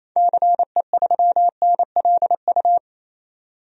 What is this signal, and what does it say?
Morse code, heard as a keyed tone, about 28 wpm, 700 Hz. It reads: CE3NLU
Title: DXCC-100 Call Signs - 6 Characters - 28wpm